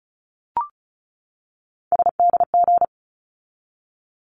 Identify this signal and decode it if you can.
Morse code, heard as a keyed tone, about 34 wpm, 700 Hz. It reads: SDG
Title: ARRL Sections 35wpm